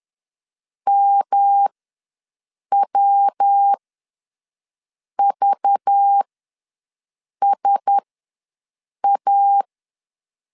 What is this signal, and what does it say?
Morse code, heard as a keyed tone, about 10 wpm, spaced as for 5 wpm, 800 Hz. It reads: MWVSA
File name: LEZIONI / 9 2 ws 10wpm 4130